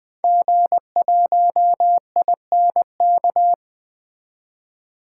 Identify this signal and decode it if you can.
Morse code, heard as a keyed tone, about 20 wpm, 700 Hz. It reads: G1INK
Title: 2023 SOTA Activator Call Signs - 20wpm